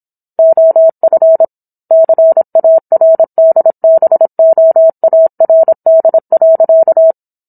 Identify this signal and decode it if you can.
Morse code, heard as a keyed tone, about 27 wpm, 650 Hz. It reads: OF CARDBOARD.
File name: Book One / 17-Chapter 17 - 26 WPM